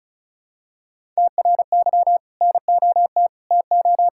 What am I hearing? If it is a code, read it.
Morse code, heard as a keyed tone, about 36 wpm, 700 Hz. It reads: TRY NOT TO
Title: Sentences from Top 1000 Words - Review Part 5 - 35wpm